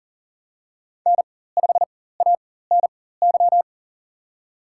Morse code, heard as a keyed tone, about 40 wpm, spaced as for 19 wpm, 700 Hz. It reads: N5ANY
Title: USA Call Signs - 5 Characters - 4x Character Spacing - 40wpm